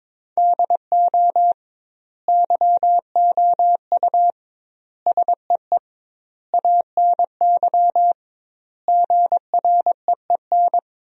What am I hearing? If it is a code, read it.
Morse code, heard as a keyed tone, about 21 wpm, 700 Hz. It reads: DO YOU SEE ANY GREEN
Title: 2x Word-Spacing - Sentences from Top 500 Words - 22wpm